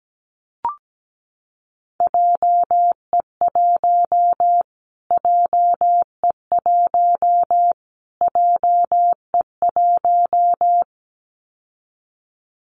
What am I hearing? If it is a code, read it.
Morse code, heard as a keyed tone, about 17 wpm, 700 Hz. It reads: JE1 JE1 JE1
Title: Callsign Prefixes for Top 25 DXCC - Repeated 3x - 17wpm